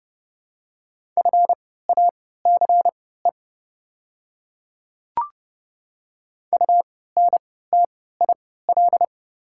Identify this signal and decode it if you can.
Morse code, heard as a keyed tone, about 32 wpm, spaced as for 17 wpm, 700 Hz. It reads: FACE UNTIL
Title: 3x Character Spacing - Top 300 Words - 30wpm